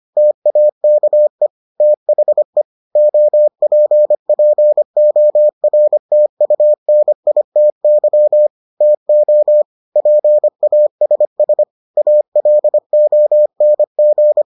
Morse code, heard as a keyed tone, about 25 wpm, 600 Hz. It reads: TAKE THE OPPORTUNITY TO PASS ALONG